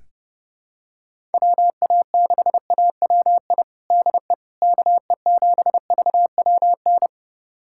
Morse code, heard as a keyed tone, about 28 wpm, 700 Hz. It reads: WA6AWI DE KE7VWN
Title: USA Callsign DE Callsign - 6 Characters - 30wpm